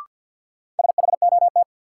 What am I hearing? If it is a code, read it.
Morse code, heard as a keyed tone, about 52 wpm, 700 Hz. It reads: SHOT
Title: Sentences from Top 1100 Words - New Word Drill Alt - 50wpm